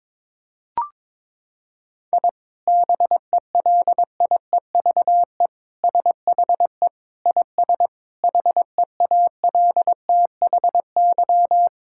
Morse code, heard as a keyed tone, about 22 wpm, 700 Hz. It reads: I BELIEVE SHE IS HEALTHY
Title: Sentences from Top 1000 Words - Review Part 5 - 22wpm